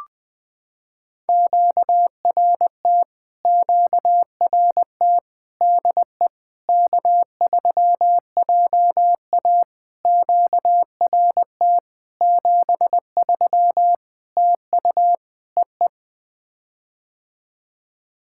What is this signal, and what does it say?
Morse code, heard as a keyed tone, about 20 wpm, 700 Hz. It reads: QRT QRT DE K3JA QRT 73 TU EE
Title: POTA QSO Practice - 20wpm